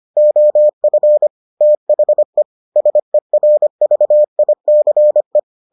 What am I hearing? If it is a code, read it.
Morse code, heard as a keyed tone, about 25 wpm, 600 Hz. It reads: OF THE SERVICE